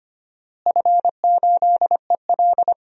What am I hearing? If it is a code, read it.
Morse code, heard as a keyed tone, about 25 wpm, 700 Hz. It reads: F8EL
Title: DXCC-100 Call Signs - 4 Characters - Repeated 3x - 25wpm